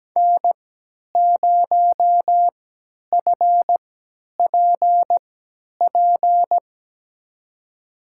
Morse code, heard as a keyed tone, about 17 wpm, spaced as for 10 wpm, 700 Hz. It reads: N0FPP
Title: USA Call Signs - 5 Characters - 3x Character Spacing - 17wpm